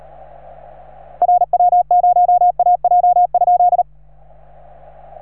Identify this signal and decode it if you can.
Morse code, heard as a keyed tone, about 36 wpm, 710 Hz. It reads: RW0AJ?
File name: RS3A-20